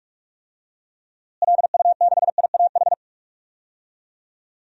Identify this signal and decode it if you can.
Morse code, heard as a keyed tone, about 45 wpm, 700 Hz. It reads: LU6IAH
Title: DXCC-100 Call Signs - 6 Characters - Repeated 2x - 45wpm